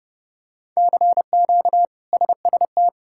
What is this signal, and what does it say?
Morse code, heard as a keyed tone, about 32 wpm, 700 Hz. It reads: CQ SST
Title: K1USN SST QSO Practice - 30wpm